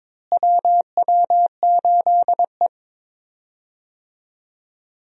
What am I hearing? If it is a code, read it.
Morse code, heard as a keyed tone, about 22 wpm, 700 Hz. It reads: WW8E
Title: USA Call Signs - 4 Characters - Repeated 2x - 22wpm